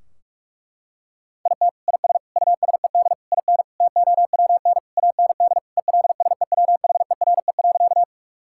Morse code, heard as a keyed tone, about 47 wpm, 700 Hz. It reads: IT IS USED IN TOWN AND ELSEWHERE.
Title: Sentences from Top 1600 Words 45wpm